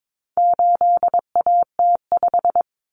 Morse code, heard as a keyed tone, about 22 wpm, 700 Hz. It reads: 8AT5
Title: Common Cut Numbers - 4-Digit 22wpm